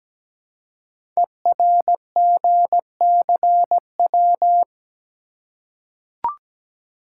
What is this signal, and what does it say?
Morse code, heard as a keyed tone, about 17 wpm, 700 Hz. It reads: ERGCW
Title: Code Groups - 5 Letter:Numbers - 17wpm